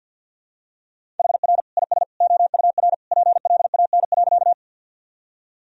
Morse code, heard as a keyed tone, about 48 wpm, 700 Hz. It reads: HR IS OUR PLAN.
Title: Sentences from Top 500 Words and Abbreviations 50wpm